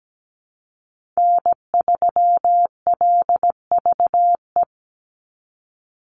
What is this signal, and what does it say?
Morse code, heard as a keyed tone, about 17 wpm, 700 Hz. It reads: N3LVE
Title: USA Call Signs 17wpm